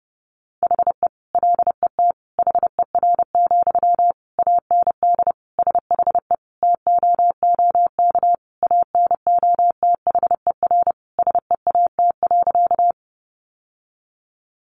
Morse code, heard as a keyed tone, about 30 wpm, 700 Hz. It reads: HE LET HER, AND SHE TOOK ANOTHER SEAT.
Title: Sentences from Top 1500 Words 30wpm